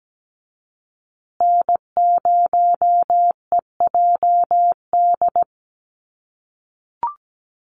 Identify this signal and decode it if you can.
Morse code, heard as a keyed tone, about 17 wpm, 700 Hz. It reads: N0EJD
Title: USA Call Signs Repeated 2x 17wpm